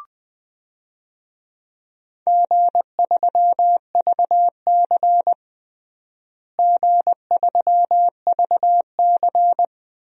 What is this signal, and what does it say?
Morse code, heard as a keyed tone, about 21 wpm, 700 Hz. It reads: G3VC G3VC
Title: DXCC-100 Call Signs - 4 Characters - Repeated 2x - 20wpm